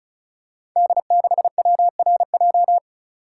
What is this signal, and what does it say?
Morse code, heard as a keyed tone, about 36 wpm, 700 Hz. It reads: D6WRJ
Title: Code Groups - 5 Letter:Numbers - 35wpm